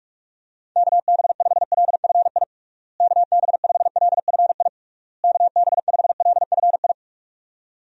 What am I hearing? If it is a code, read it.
Morse code, heard as a keyed tone, about 41 wpm, 700 Hz. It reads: KB5LFI KB5LFI KB5LFI
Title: USA Call Signs - 6 Characters - Repeated 3x - 45wpm